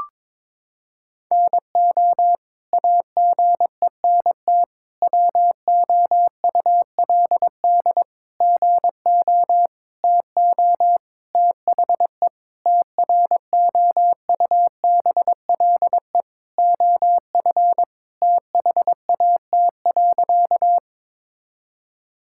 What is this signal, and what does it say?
Morse code, heard as a keyed tone, about 22 wpm, 700 Hz. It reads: NO AGENT WOULD GO TO THE TROUBLE OF THAT.
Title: Sentences from Top 1600 Words 22wpm